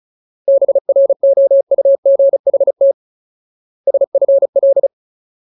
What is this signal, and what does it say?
Morse code, heard as a keyed tone, about 34 wpm, 550 Hz. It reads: BROUGHT SFL
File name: Ultimate Word List / 35WPM Ultimate